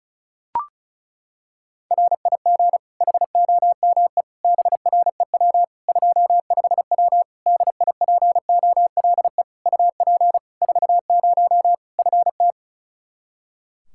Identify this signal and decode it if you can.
Morse code, heard as a keyed tone, about 34 wpm, 700 Hz. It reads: RIG HOME BREW 25W DIPOLE UP 40 FT
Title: K1USN SST QSO Practice 35wpm